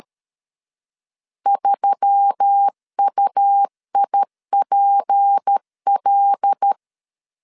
Morse code, heard as a keyed tone, about 13 wpm, 800 Hz. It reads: 3UIPL
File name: LEZIONI / 6 PKX0 / 6 P 12wpm 2033